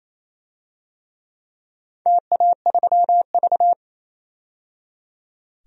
Morse code, heard as a keyed tone, about 28 wpm, 700 Hz. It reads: TA3V
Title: DXCC-100 Call Signs - 4 Characters - 28wpm